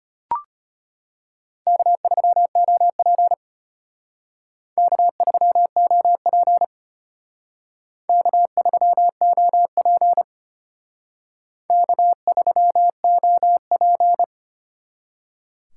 Morse code, heard as a keyed tone, about 32 wpm, 700 Hz. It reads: K3OP K3OP K3OP K3OP